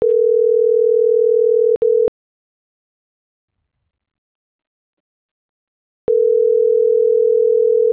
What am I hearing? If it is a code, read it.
Ringback tone (North America), 440 + 480 Hz.